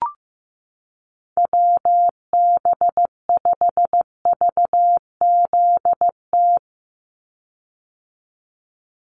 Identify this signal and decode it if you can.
Morse code, heard as a keyed tone, about 15 wpm, 700 Hz. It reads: WB5VZT
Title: USA Call Signs 15wpm